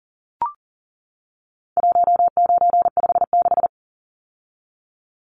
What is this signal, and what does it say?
Morse code, heard as a keyed tone, about 40 wpm, 700 Hz. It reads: 1956
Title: Numbers - 4-Digit 40wpm